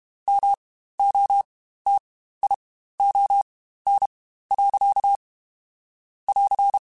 Morse code, heard as a keyed tone, about 32 wpm, spaced as for 15 wpm, 800 Hz. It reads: MOTION. <AR>
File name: Spaced / Chuckles Spaced-mp3s / BrainMouth sp @30